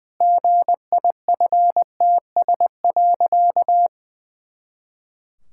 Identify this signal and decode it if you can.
Morse code, heard as a keyed tone, about 20 wpm, 700 Hz. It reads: GIFTS.
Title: Pangram Challenge v2 - 20wpm